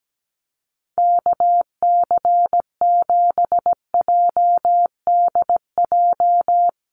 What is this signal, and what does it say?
Morse code, heard as a keyed tone, about 17 wpm, 700 Hz. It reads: KC7JDJ
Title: USA Call Signs Repeated 2x 17wpm